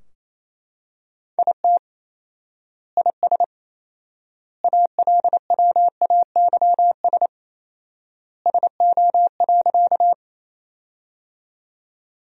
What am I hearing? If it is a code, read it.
Morse code, heard as a keyed tone, about 28 wpm, 700 Hz. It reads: IT IS ALWAYS SO.
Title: 4x Word-Spacing - Sentences from Top 300 Words - 28wpm